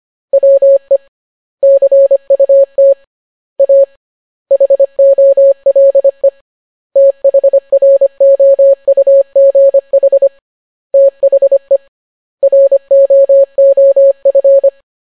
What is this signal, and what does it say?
Morse code, heard as a keyed tone, about 25 wpm, 550 Hz. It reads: WE CUT A HOLE THROUGH THE ROOF